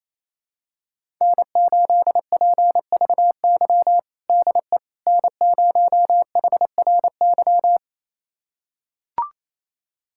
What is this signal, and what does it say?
Morse code, heard as a keyed tone, about 27 wpm, 700 Hz. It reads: N8PVY DE N0HRY
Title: USA Callsign DE Callsign - 5 Characters - 28wpm